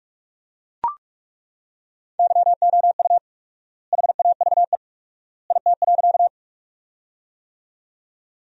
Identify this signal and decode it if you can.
Morse code, heard as a keyed tone, about 41 wpm, 700 Hz. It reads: YOU HAVE IT.
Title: 4x Word-Spacing - Sentences from Top 100 Words - 45wpm